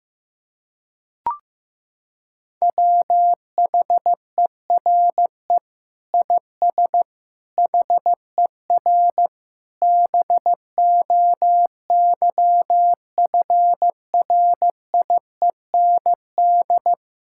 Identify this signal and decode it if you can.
Morse code, heard as a keyed tone, about 15 wpm, 700 Hz. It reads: WHERE IS HER BOYFRIEND
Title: Sentences from Top 1000 Words - Review Part 5 - 15wpm